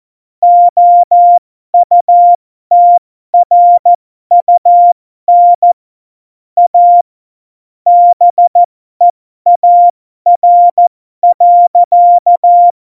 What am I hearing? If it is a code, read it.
Morse code, heard as a keyed tone, about 14 wpm, spaced as for 12 wpm, 700 Hz. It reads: OUTRUN A BEAR.
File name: Morsekurs / Witze